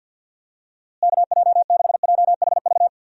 Morse code, heard as a keyed tone, about 50 wpm, 700 Hz. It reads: KJ6JHV
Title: USA Call Signs - 6 Characters - Repeated 2x - 50wpm